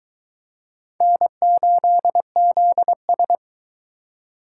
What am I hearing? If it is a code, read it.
Morse code, heard as a keyed tone, about 20 wpm, 700 Hz. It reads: N8ZS